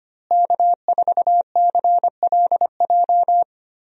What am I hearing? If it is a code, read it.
Morse code, heard as a keyed tone, about 25 wpm, 700 Hz. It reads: K4CLJ
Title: USA Call Signs - 5 Characters - Repeated 3x - 25wpm